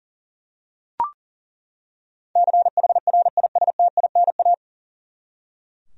Morse code, heard as a keyed tone, about 40 wpm, 700 Hz. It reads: CHRISTINA
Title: USA Common Names - 40wpm